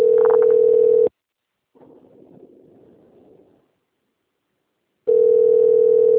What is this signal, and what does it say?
Ringback tone (North America), 440 + 480 Hz.
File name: out- - - - - .2549